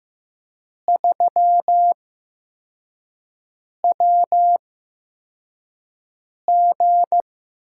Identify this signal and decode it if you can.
Morse code, heard as a keyed tone, about 15 wpm, spaced as for 4 wpm, 700 Hz. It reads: 3WG
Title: USA Call Signs - 8x Character Spacing - 15wpm